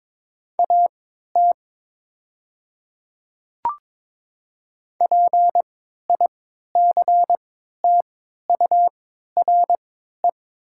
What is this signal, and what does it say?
Morse code, heard as a keyed tone, about 22 wpm, spaced as for 13 wpm, 700 Hz. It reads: AT PICTURE
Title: 3x Character Spacing - Top 200 Words - 22wpm